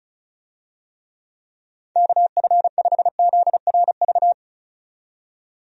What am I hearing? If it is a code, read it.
Morse code, heard as a keyed tone, about 36 wpm, 700 Hz. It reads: KF5ZRV
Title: USA Call Signs - 6 Characters - 35wpm